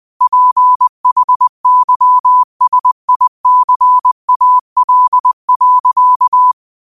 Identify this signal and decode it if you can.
Morse code, heard as a keyed tone, about 20 wpm, 1000 Hz. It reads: PHYSICAL.
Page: Quotes for Thu, 14 Aug 2025 in Morse Code at 20 words per minute.